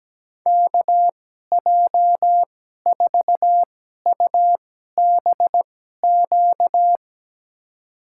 Morse code, heard as a keyed tone, about 17 wpm, spaced as for 12 wpm, 700 Hz. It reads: KJ4UBQ
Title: USA Call Signs - 6 Characters - 2x Character Spacing - 17wpm